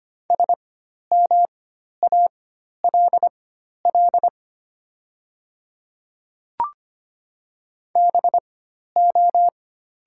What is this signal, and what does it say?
Morse code, heard as a keyed tone, about 25 wpm, spaced as for 12 wpm, 700 Hz. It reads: SMALL BO
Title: 4x Character Spacing - Top 300 Words - 25wpm